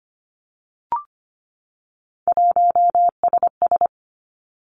Morse code, heard as a keyed tone, about 25 wpm, 700 Hz. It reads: 1SS
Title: Three Challenging Letter-Number Combinations 25wpm